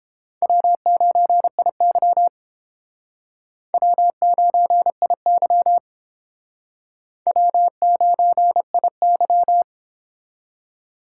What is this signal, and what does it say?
Morse code, heard as a keyed tone, about 30 wpm, 700 Hz. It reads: W9IY W9IY W9IY